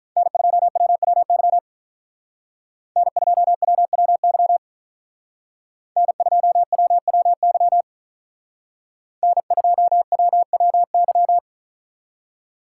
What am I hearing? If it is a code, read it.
Morse code, heard as a keyed tone, about 45 wpm, 700 Hz. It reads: N2WWY N2WWY N2WWY N2WWY